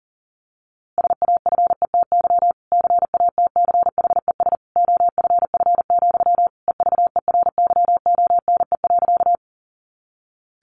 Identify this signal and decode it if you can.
Morse code, heard as a keyed tone, about 40 wpm, 700 Hz. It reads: SAFETY CATCHES OFF, EVERYONE.
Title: A Sound Of Thunder 40wpm